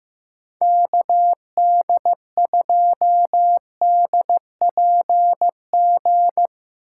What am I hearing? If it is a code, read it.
Morse code, heard as a keyed tone, about 15 wpm, 700 Hz. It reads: KD2DPG
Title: USA Call Signs - 6 Characters - 8x Character Spacing - 15wpm